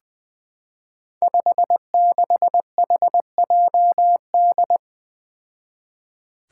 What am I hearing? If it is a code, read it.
Morse code, heard as a keyed tone, about 20 wpm, 700 Hz. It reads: 56HJD
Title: Five Challenging Letter-Number Combinations 20wpm